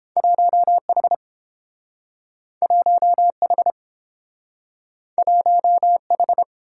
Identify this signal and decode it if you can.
Morse code, heard as a keyed tone, about 30 wpm, 700 Hz. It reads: 1H 1H 1H